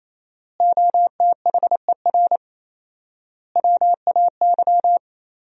Morse code, heard as a keyed tone, about 28 wpm, 700 Hz. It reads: OTHER WAY
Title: 4x Word-Spacing - Sentences from Top 200 Words - 28wpm